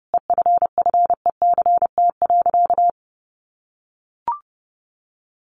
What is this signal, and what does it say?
Morse code, heard as a keyed tone, about 30 wpm, 700 Hz. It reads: EFFECT.
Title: Sentences from Top 1000 Words 30wpm